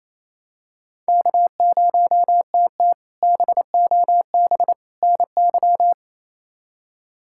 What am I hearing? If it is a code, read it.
Morse code, heard as a keyed tone, about 28 wpm, 700 Hz. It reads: K0TT BOB NY
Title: K1USN SST QSO Practice - 28wpm